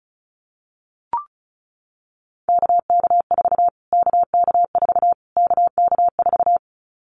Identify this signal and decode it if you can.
Morse code, heard as a keyed tone, about 34 wpm, 700 Hz. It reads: KK4 KK4 KK4
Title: Callsign Prefixes for Top 25 DXCC - Repeated 3x - 35wpm